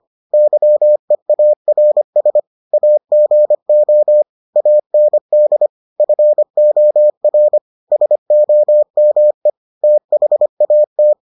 Morse code, heard as a keyed tone, about 25 wpm, 600 Hz. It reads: YEARS AGO AND FOR SOME THAT